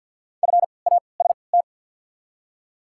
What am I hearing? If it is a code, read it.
Morse code, heard as a keyed tone, about 47 wpm, spaced as for 28 wpm, 700 Hz. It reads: FAST
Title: 3x Character Spacing - Top 500 Words - 50wpm